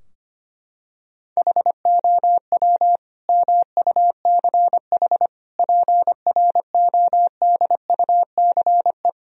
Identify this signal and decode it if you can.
Morse code, heard as a keyed tone, about 25 wpm, 700 Hz. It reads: HOW MUCH PRODUCE
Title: Sentences from Top 1000 Words - Review Part 3 - 25wpm